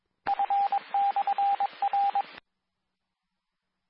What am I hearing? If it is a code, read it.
Morse code, heard as a keyed tone, about 22 wpm, 800 Hz, with radio noise. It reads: F/R
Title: Nassau and Suffolk County Amateur Repeaters